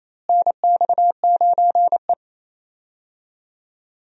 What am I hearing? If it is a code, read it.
Morse code, heard as a keyed tone, about 28 wpm, 700 Hz. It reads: NX9E
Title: USA Call Signs - 4 Characters - Repeated 3x - 28wpm